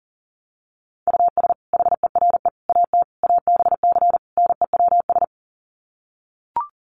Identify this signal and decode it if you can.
Morse code, heard as a keyed tone, about 40 wpm, 700 Hz. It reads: US HERE AT ABC NEWS
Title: Sets of 5 Words 40wpm